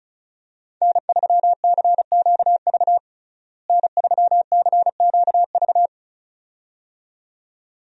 Morse code, heard as a keyed tone, about 34 wpm, 700 Hz. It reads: N3CQV N3CQV
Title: USA Call Signs - 5 Characters - Repeated 2x - 35wpm